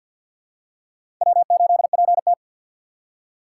Morse code, heard as a keyed tone, about 52 wpm, 700 Hz. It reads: W8PT
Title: USA Call Signs - 4x Character Spacing - 50wpm